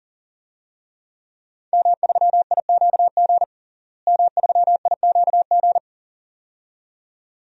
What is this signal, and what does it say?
Morse code, heard as a keyed tone, about 37 wpm, 700 Hz. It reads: M3IQG M3IQG
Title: DXCC-100 Call Signs - 5 Characters - Repeated 2x - 40wpm